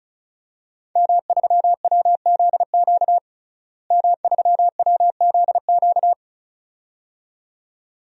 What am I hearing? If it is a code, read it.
Morse code, heard as a keyed tone, about 34 wpm, 700 Hz. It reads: M3WZQ M3WZQ
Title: DXCC-100 Call Signs - 5 Characters - Repeated 2x - 35wpm